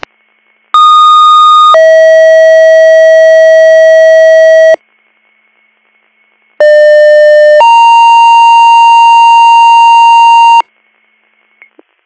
Two-tone page, heard as a one-second tone, then a three-second tone, about 1230 then 640 Hz, about 600 then 930 Hz. It is heard from 0.7 to 10.6 s.